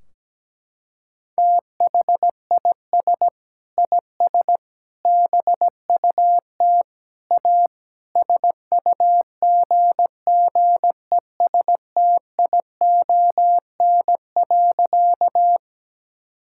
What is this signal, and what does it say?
Morse code, heard as a keyed tone, about 17 wpm, 700 Hz. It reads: THIS IS BUT A SUGGESTION.
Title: Sentences from Top 1600 Words 17wpm